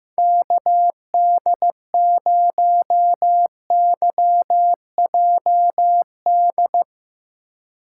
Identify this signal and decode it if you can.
Morse code, heard as a keyed tone, about 15 wpm, 700 Hz. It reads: KD0YJD
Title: USA Call Signs - 6 Characters - 3x Character Spacing - 15wpm